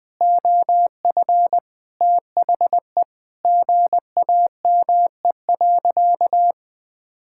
Morse code, heard as a keyed tone, about 20 wpm, 700 Hz. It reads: OF THE GAME.
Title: Sentences with 5-Word Phrases - 20wpm